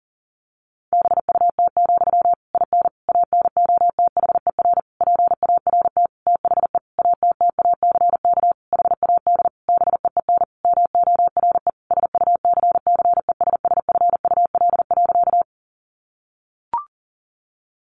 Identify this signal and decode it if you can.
Morse code, heard as a keyed tone, about 40 wpm, 700 Hz. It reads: BUT, IN ANOTHER PART THE ATTACK HAD BEEN MORE SUCCESSFUL.
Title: Sentences from Top 1500 Words 40wpm